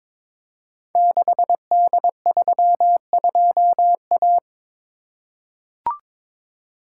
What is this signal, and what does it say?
Morse code, heard as a keyed tone, about 23 wpm, 700 Hz. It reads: 6D32A
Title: Code Groups - 5 Letter:Numbers - 22wpm